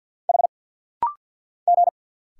Morse code, heard as a keyed tone, about 46 wpm, 690 Hz. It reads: H G
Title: Single Letters - Rapid-Fire 50wpm